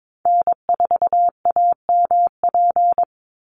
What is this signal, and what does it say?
Morse code, heard as a keyed tone, about 22 wpm, 700 Hz. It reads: N4AMP
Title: USA Call Signs 22wpm